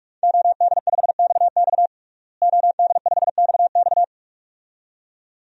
Morse code, heard as a keyed tone, about 44 wpm, 700 Hz. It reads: OD5XX OD5XX
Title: DXCC-100 Call Signs - 5 Characters - Repeated 2x - 45wpm